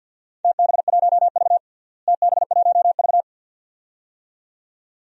Morse code, heard as a keyed tone, about 53 wpm, 700 Hz. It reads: TB1V TB1V
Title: DXCC-100 Call Signs - 4 Characters - Repeated 2x - 50wpm